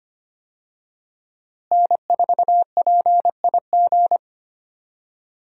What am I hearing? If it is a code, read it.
Morse code, heard as a keyed tone, about 25 wpm, 700 Hz. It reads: N4PIG
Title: USA Call Signs - 5 Characters - Repeated 2x - 25wpm